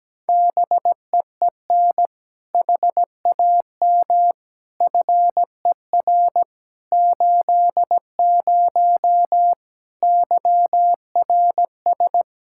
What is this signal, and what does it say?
Morse code, heard as a keyed tone, about 17 wpm, 700 Hz. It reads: BEEN HAM FER 80 YRS
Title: K1USN SST QSO Practice 17wpm